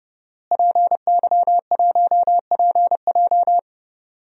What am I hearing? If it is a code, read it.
Morse code, heard as a keyed tone, about 29 wpm, 700 Hz. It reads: PY1PJ
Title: DXCC-100 Call Signs - 5 Characters - Repeated 3x - 30wpm